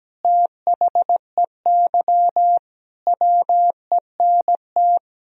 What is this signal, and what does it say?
Morse code, heard as a keyed tone, about 17 wpm, 700 Hz. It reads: THEY WENT
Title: Sentences from Top 1000 Words - Review Part 1 - 17wpm